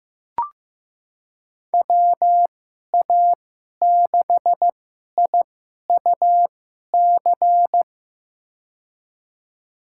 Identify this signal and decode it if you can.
Morse code, heard as a keyed tone, about 15 wpm, spaced as for 11 wpm, 700 Hz. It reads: WA6IUC
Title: USA Call Signs - 6 Characters - 2x Character Spacing - 15wpm